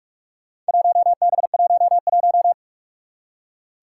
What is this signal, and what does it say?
Morse code, heard as a keyed tone, about 41 wpm, 700 Hz. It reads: 1B11
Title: Four Challenging Letter-Number Combinations 45wpm